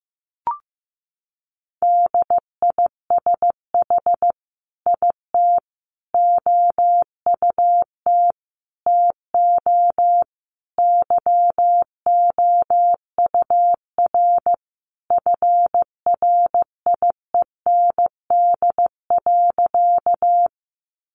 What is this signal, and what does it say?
Morse code, heard as a keyed tone, about 15 wpm, 700 Hz. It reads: DISH IT OUT TO YOUR FRIEND.
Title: Sentences from Top 1300 Words 15wpm